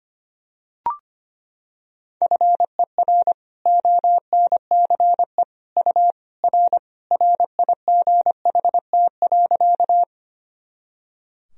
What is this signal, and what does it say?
Morse code, heard as a keyed tone, about 25 wpm, 700 Hz. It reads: FER ONCE U R RIGHT.
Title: Sentences from Top 300 Words and Abbreviations 25wpm